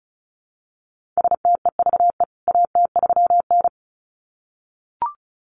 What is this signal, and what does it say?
Morse code, heard as a keyed tone, about 34 wpm, 700 Hz. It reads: STEVE AT3N
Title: CWT Exchange 35wpm